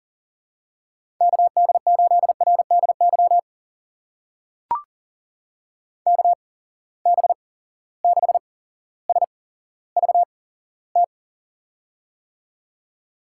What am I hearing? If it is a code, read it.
Morse code, heard as a keyed tone, about 40 wpm, spaced as for 17 wpm, 700 Hz. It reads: KD8RDY KB6SVT
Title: USA Call Signs - 6 Characters - 8x Character Spacing - 40wpm